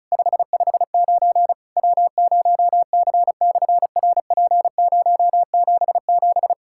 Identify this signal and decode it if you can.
Morse code, heard as a keyed tone, about 36 wpm, 700 Hz. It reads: 559 W0C/RP077
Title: SOTA QSO Practice 35wpm